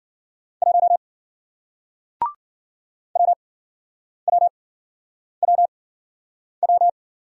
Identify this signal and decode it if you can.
Morse code, heard as a keyed tone, about 55 wpm, spaced as for 9 wpm, 700 Hz. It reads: 1 WWWW